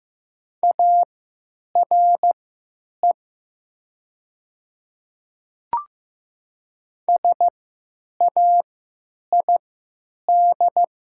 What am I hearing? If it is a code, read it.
Morse code, heard as a keyed tone, about 15 wpm, spaced as for 9 wpm, 700 Hz. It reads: ARE SAID
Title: 3x Character Spacing - Top 200 Words - 15wpm